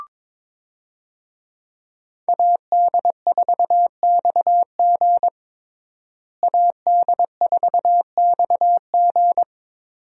Morse code, heard as a keyed tone, about 22 wpm, 700 Hz. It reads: AD4XG AD4XG
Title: USA Call Signs - 5 Characters - Repeated 2x - 22wpm